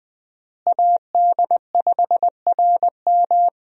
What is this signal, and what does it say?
Morse code, heard as a keyed tone, about 20 wpm, 700 Hz. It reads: AD5RM
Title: USA Call Signs - 5 Characters - Repeated 2x - 20wpm